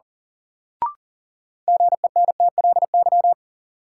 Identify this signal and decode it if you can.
Morse code, heard as a keyed tone, about 37 wpm, 700 Hz. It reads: GENTLY
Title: Sentences from Top 1500 Words - New Word Drill Alt - 40wpm